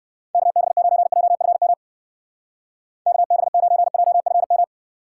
Morse code, heard as a keyed tone, about 55 wpm, 700 Hz. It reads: KB9JFM KB9JFM
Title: Speed Racing - USA Call Signs 45wpm